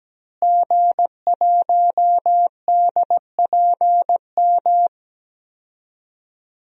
Morse code, heard as a keyed tone, about 17 wpm, 700 Hz. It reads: G1DPM
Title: DXCC-100 Call Signs - 5 Characters - Repeated 3x - 17wpm